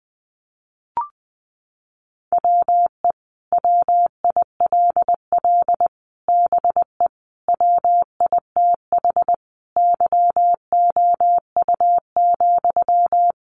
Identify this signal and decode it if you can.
Morse code, heard as a keyed tone, about 20 wpm, 700 Hz. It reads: WE WILL BE WITH YOU,
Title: Sentences from Top 600 Words 20wpm